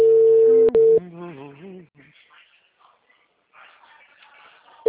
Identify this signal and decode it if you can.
Ringback tone (China, 450 Hz), with speech.